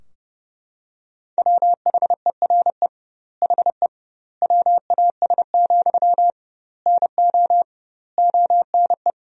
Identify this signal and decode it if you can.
Morse code, heard as a keyed tone, about 32 wpm, 700 Hz. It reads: WHERE HE WAS, NO ONE
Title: 2x Word-Spacing - Sentences from Top 400 Words - 30wpm